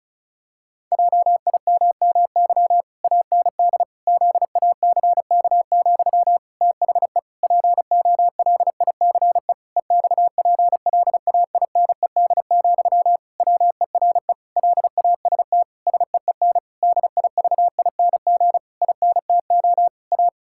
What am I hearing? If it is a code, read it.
Morse code, heard as a keyed tone, about 34 wpm, 700 Hz. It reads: JIMMY AND ZACK, THE POLICE EXPLAINED, WERE LAST SEEN DIVING INTO A
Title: Pangram Challenge v2 - 35wpm